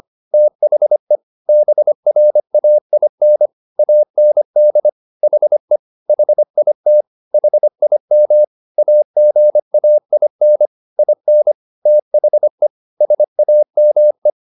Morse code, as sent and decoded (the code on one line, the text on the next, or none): - .... / -... .-. .- .. -. / .- -. -.. / .... / .... .. - / .... .. -- / .- --. .- .. -. / .. -. / - .... / ... .- -- .
THE BRAIN AND HE HIT HIM AGAIN IN THE SAME